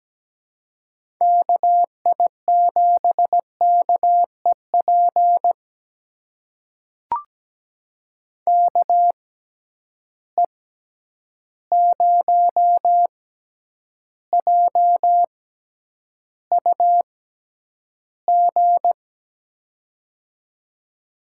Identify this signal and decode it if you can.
Morse code, heard as a keyed tone, about 17 wpm, spaced as for 9 wpm, 700 Hz. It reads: KI7KEP KE0JUG
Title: USA Call Signs - 6 Characters - 6x Character Spacing - 17wpm